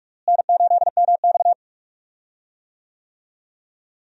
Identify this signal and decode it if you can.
Morse code, heard as a keyed tone, about 46 wpm, 700 Hz. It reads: N8MX
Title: USA Call Signs - 4 Characters - Repeated 2x - 45wpm